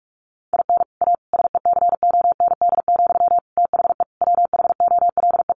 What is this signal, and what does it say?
Morse code, heard as a keyed tone, about 44 wpm, 700 Hz. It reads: IN A SECOND, THE WHOLE
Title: Sentences from Top 1500 Words 45wpm